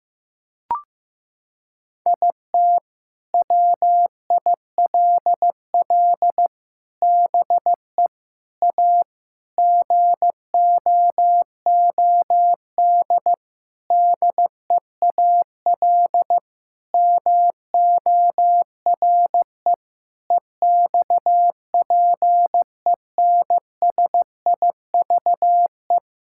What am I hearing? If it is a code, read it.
Morse code, heard as a keyed tone, about 15 wpm, 700 Hz. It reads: IT WILL BE A GOOD DEAL MORE EXPENSIVE
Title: Sentences from Top 1000 Words - Review Part 6 - 15wpm